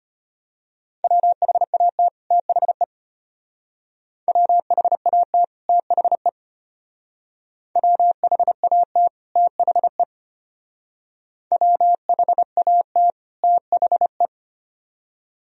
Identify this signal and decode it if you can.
Morse code, heard as a keyed tone, about 31 wpm, 700 Hz. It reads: WHAT THE WHAT THE WHAT THE WHAT THE
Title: Speed-Racing - Sets of 2 Words 25wpm